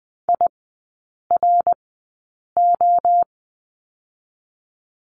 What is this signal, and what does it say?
Morse code, heard as a keyed tone, about 20 wpm, spaced as for 8 wpm, 700 Hz. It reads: IRO
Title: 074 Lesson 13e - Letter L - Three Characters 20wpm